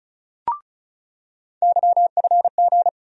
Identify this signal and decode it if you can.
Morse code, heard as a keyed tone, about 36 wpm, 700 Hz. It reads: YFG
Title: USA Call Sign Postfixes - 3 Letters 35wpm